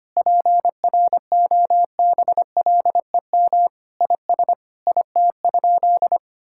Morse code, heard as a keyed tone, about 25 wpm, 700 Hz. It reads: PROBLEM IS IT?
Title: Sentences from Top 1000 Words - Review Part 6 - 25wpm